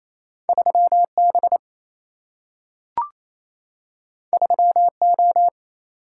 Morse code, heard as a keyed tone, about 28 wpm, 700 Hz. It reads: 3B 3O
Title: Single-Letter Single-Number 28wpm